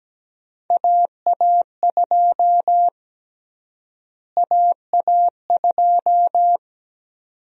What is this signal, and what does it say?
Morse code, heard as a keyed tone, about 17 wpm, 700 Hz. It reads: AA2 AA2
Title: USA Call Sign Prefixes - Repeated 2x - 17wpm